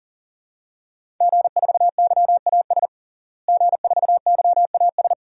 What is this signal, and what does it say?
Morse code, heard as a keyed tone, about 40 wpm, 700 Hz. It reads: G4YAS G4YAS
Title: DXCC-100 Call Signs - 5 Characters - Repeated 2x - 40wpm